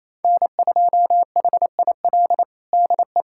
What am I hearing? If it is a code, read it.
Morse code, heard as a keyed tone, about 28 wpm, 700 Hz. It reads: N2HIL DE
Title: USA Callsign DE Callsign - 5 Characters - 28wpm